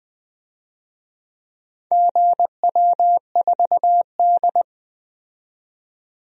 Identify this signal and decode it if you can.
Morse code, heard as a keyed tone, about 20 wpm, 700 Hz. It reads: GW4D
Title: DXCC-100 Call Signs - 4 Characters - 20wpm